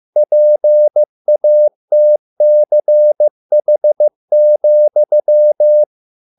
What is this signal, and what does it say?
Morse code, heard as a keyed tone, about 15 wpm, 600 Hz. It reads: PATCH,